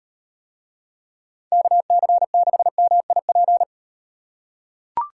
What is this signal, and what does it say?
Morse code, heard as a keyed tone, about 38 wpm, 700 Hz. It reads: KC6MIP